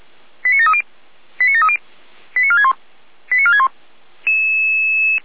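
Five-tone selective call (ZVEI), heard as quick runs of five tones, tones of about 70 ms. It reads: 89730 89730 89461 89461
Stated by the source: ZVEI five-tone call received by SDR, recorded by Feuerwehr Orlamünde.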